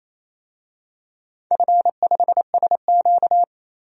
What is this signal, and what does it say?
Morse code, heard as a keyed tone, about 28 wpm, 690 Hz. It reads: F5SQ
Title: DXCC-100 Call Signs - 4 Characters - 28wpm